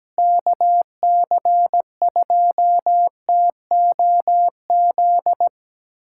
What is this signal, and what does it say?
Morse code, heard as a keyed tone, about 17 wpm, 700 Hz. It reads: KC2TOZ
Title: USA Callsign DE Callsign 17wpm